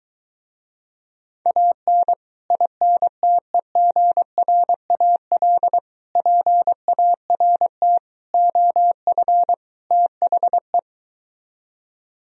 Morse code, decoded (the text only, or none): AN INTEGRAL PART OF THE